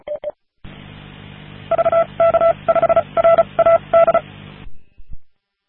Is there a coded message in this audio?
Yes: Morse code from 1.7 to 4.2 s, heard as a keyed tone, about 34 wpm, spaced as for 27 wpm, 670 Hz. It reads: VK5RAD